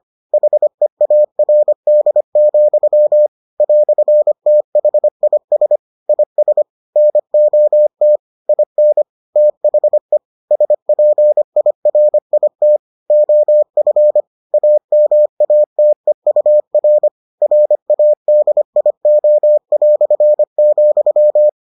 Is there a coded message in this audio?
Yes: Morse code from 0.3 to 21.5 s, heard as a keyed tone, about 25 wpm, 600 Hz. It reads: HEARD, "THIS IS NOT IN THE SPIRIT OF AMATEUR RADIO",